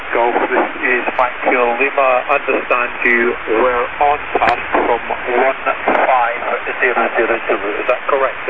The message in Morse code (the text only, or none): none